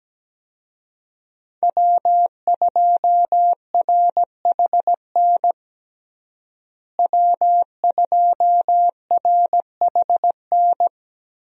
Morse code, heard as a keyed tone, about 17 wpm, 700 Hz. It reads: W2RHN W2RHN
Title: USA Call Signs - 5 Characters - Repeated 2x - 17wpm